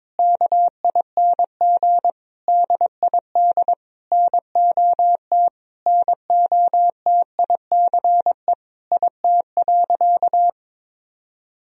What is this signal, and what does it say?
Morse code, heard as a keyed tone, about 22 wpm, 700 Hz. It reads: KING DID NOT NOTICE IT.
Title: Sentences from Top 1000 Words - Review Part 2 - 22wpm